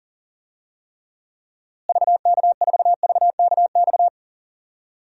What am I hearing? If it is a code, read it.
Morse code, heard as a keyed tone, about 40 wpm, 700 Hz. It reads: VK4VKX
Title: DXCC-100 Call Signs - 6 Characters - Repeated 2x - 40wpm